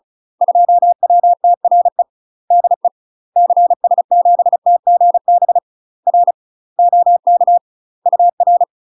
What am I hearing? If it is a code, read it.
Morse code, heard as a keyed tone, about 36 wpm, 700 Hz. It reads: 2WTRE DE CS7TGB R OK UR
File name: UK MASTER QSOs TRAINING / 35WPM / UK MASTER QSOs 0028